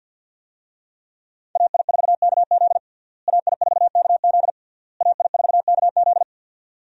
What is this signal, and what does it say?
Morse code, heard as a keyed tone, about 51 wpm, 700 Hz. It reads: AI4KZ AI4KZ AI4KZ
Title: USA Call Signs - 5 Characters - Repeated 3x - 50wpm